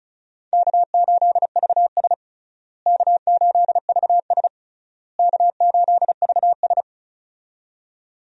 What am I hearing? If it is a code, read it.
Morse code, heard as a keyed tone, about 36 wpm, 700 Hz. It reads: K8VS K8VS K8VS
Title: USA Call Signs - 4 Characters - Repeated 3x - 35wpm